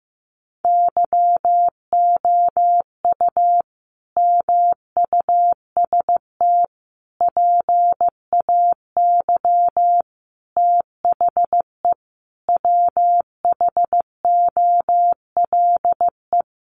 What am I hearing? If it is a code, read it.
Morse code, heard as a keyed tone, about 15 wpm, 700 Hz. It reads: YOU MUST PAY THE WHOLE
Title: Sentences from Top 1400 Words 15wpm